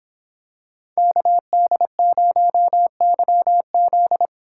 Morse code, heard as a keyed tone, about 24 wpm, 700 Hz. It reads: KD0YZ